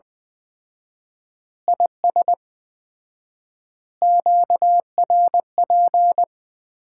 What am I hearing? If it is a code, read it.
Morse code, heard as a keyed tone, about 20 wpm, 700 Hz. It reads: IS QRP
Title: Common QSO Phrases - 4x Word-Spacing - 20wpm